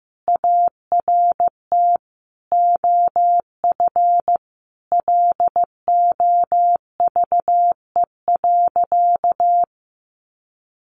Morse code, heard as a keyed tone, about 15 wpm, 700 Hz. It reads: ART OF LOVE.
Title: Sentences from Top 1500 Words 15wpm